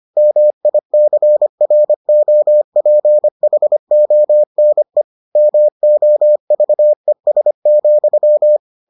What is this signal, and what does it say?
Morse code, heard as a keyed tone, about 25 wpm, 600 Hz. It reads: MICROPHONE MOVES,